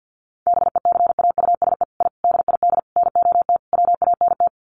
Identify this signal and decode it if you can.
Morse code, heard as a keyed tone, about 50 wpm, 700 Hz. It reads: BECAUSE I DID NOT WANT
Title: Sets of 5 Words Encore2 50wpm